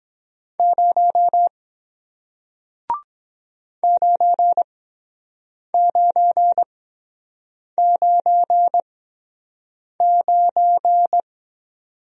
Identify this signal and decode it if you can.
Morse code, heard as a keyed tone, about 23 wpm, spaced as for 7 wpm, 700 Hz. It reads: 0 9999